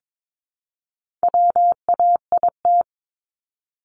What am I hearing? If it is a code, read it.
Morse code, heard as a keyed tone, about 22 wpm, 700 Hz. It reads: WAIT
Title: ING Suffix - Top 100 Words with Base Words 22wpm